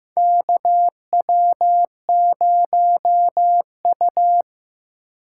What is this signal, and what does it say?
Morse code, heard as a keyed tone, about 15 wpm, 700 Hz. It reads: KW0U
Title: USA Call Signs - 4 Characters - 3x Character Spacing - 15wpm